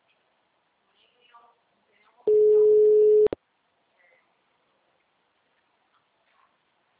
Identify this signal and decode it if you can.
Ringback tone (Europe, 425 Hz).